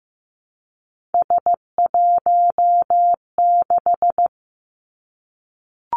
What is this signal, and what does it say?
Morse code, heard as a keyed tone, about 15 wpm, 700 Hz. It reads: S16
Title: Three Challenging Letter-Number Combinations 15wpm